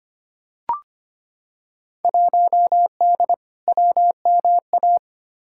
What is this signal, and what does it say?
Morse code, heard as a keyed tone, about 25 wpm, 700 Hz. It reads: 1D WMA
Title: Field Day Exchanges - 25wpm